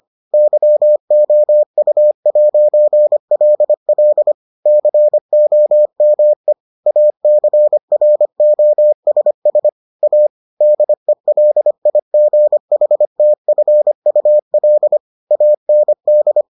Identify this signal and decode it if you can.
Morse code, heard as a keyed tone, about 25 wpm, 600 Hz. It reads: YOU'LL COME ACROSS A DELIGHTFUL AND